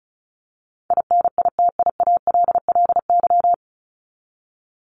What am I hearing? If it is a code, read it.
Morse code, heard as a keyed tone, about 34 wpm, 700 Hz. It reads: INITIALLY
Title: Top 1251-1500 Words 35wpm